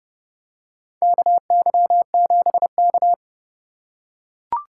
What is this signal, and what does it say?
Morse code, heard as a keyed tone, about 32 wpm, 700 Hz. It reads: KY7K
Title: K1USN SST QSO Practice - 30wpm